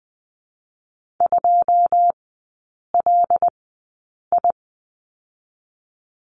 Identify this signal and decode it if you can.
Morse code, heard as a keyed tone, about 20 wpm, 700 Hz. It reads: 2 L I
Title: 114 Lesson 19e - Letter w - Three Characters 20wpm